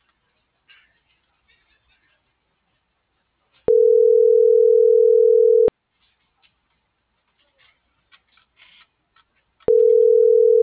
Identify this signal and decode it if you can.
Ringback tone (North America), 440 + 480 Hz.